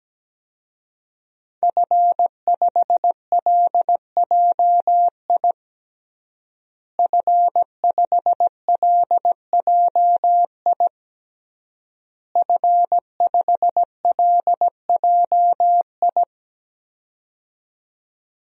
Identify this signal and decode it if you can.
Morse code, heard as a keyed tone, about 17 wpm, 700 Hz. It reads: F5LJI F5LJI F5LJI
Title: DXCC-100 Call Signs - 5 Characters - Repeated 3x - 17wpm